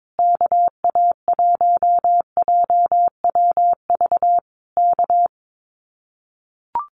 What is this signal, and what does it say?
Morse code, heard as a keyed tone, about 22 wpm, 700 Hz. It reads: KA1JWV K
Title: Common QSO Phrases Encore 22wpm